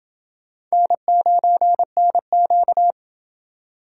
Morse code, heard as a keyed tone, about 29 wpm, 700 Hz. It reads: N9NQ